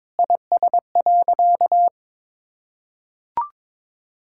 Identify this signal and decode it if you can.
Morse code, heard as a keyed tone, about 22 wpm, 700 Hz. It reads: IS.
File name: Sentences with 9-Word Phrases - 22wpm